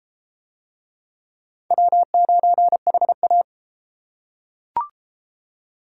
Morse code, heard as a keyed tone, about 31 wpm, 700 Hz. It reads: W9HA